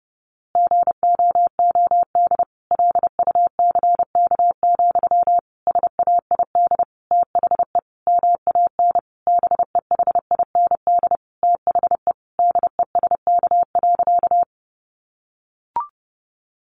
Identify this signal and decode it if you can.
Morse code, heard as a keyed tone, about 30 wpm, 700 Hz. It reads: GOOD LUCK, SAID THE MAN BEHIND THE DESK.
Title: A Sound Of Thunder 30wpm